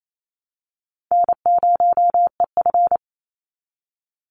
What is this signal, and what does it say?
Morse code, heard as a keyed tone, about 28 wpm, 700 Hz. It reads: N0EF
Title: Easy Call Signs Repeated 3x - 28wpm